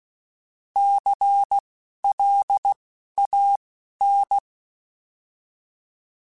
Morse code, heard as a keyed tone, about 16 wpm, spaced as for 11 wpm, 800 Hz. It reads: CLAN
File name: Alphabetical / Alphabetical mp3s / Cs @10